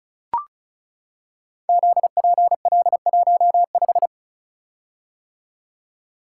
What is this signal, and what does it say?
Morse code, heard as a keyed tone, about 36 wpm, 700 Hz. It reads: ZPL15
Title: Code Groups - 5 Letter:Numbers - 35wpm